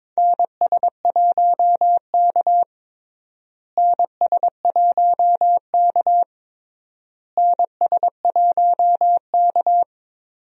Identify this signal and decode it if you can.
Morse code, heard as a keyed tone, about 22 wpm, 700 Hz. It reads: NS1K NS1K NS1K
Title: USA Call Signs - 4 Characters - Repeated 3x - 22wpm